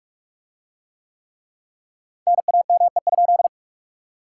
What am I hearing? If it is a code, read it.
Morse code, heard as a keyed tone, about 41 wpm, 700 Hz. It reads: NAME?